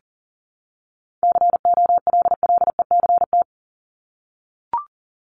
Morse code, heard as a keyed tone, about 40 wpm, 700 Hz. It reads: COLLECT
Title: Top 1501-1750 Words 40wpm